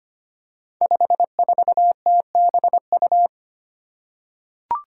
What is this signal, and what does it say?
Morse code, heard as a keyed tone, about 25 wpm, 700 Hz. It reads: 54TBU
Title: Five Challenging Letter-Number Combinations 25wpm